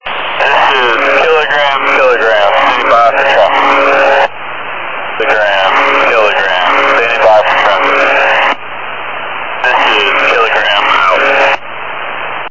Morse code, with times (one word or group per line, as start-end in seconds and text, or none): none